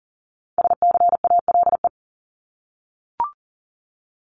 Morse code, heard as a keyed tone, about 40 wpm, 700 Hz. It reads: SCALE
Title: Top 751-1000 Words 40wpm